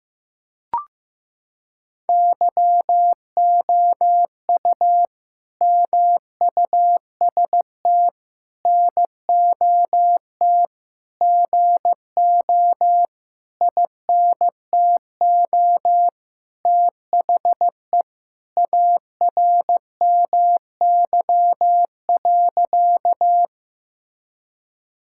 Morse code, heard as a keyed tone, about 15 wpm, 700 Hz. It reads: YOU MUST NOT GO INTO THE ARMY.
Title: Sentences from Top 1000 Words - Review Part 5 - 15wpm